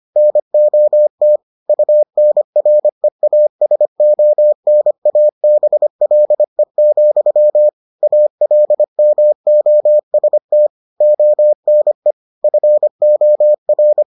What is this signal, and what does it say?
Morse code, heard as a keyed tone, about 25 wpm, 600 Hz. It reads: NOT UNREASONABLE, ALMOST ONE FOR